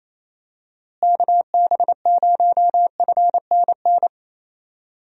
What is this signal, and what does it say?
Morse code, heard as a keyed tone, about 28 wpm, 700 Hz. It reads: KB0FNN
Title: USA Call Signs - 6 Characters - Repeated 2x - 28wpm